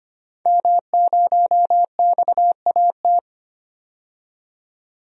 Morse code, heard as a keyed tone, about 25 wpm, 700 Hz. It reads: M0XAT
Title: DXCC-100 Call Signs - 5 Characters - 25wpm